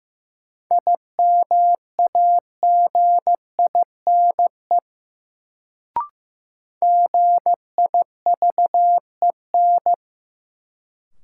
Morse code, heard as a keyed tone, about 15 wpm, 700 Hz. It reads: IMAGINE GIVEN
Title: Sentences from Top 1400 Words - New Word Drill Alt - 15wpm